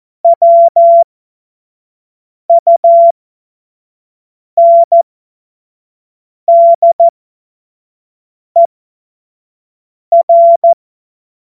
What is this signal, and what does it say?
Morse code, heard as a keyed tone, about 14 wpm, spaced as for 5 wpm, 680 Hz. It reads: WUNDER
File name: Morsekurs / Witze / Witz 29. November 2022 25 F70